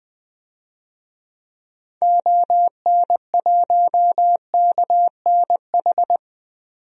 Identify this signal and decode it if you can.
Morse code, heard as a keyed tone, about 20 wpm, 700 Hz. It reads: ON1KNH
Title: DXCC-100 Call Signs - 6 Characters - 20wpm